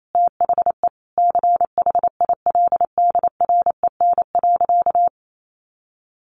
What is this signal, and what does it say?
Morse code, heard as a keyed tone, about 28 wpm, 700 Hz. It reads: THE CHILDREN.
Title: Sentences from Top 300 Words 28wpm